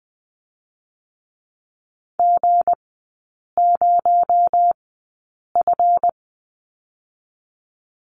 Morse code, heard as a keyed tone, about 20 wpm, spaced as for 8 wpm, 700 Hz. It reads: G0F
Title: 218 Lesson 35e - Number 0 - Three Characters 20wpm